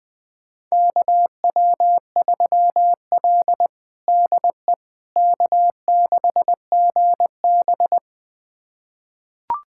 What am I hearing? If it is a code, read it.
Morse code, heard as a keyed tone, about 20 wpm, 700 Hz. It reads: KW3L DE K6GB
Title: USA Callsign DE Callsign - 4 Characters - 20wpm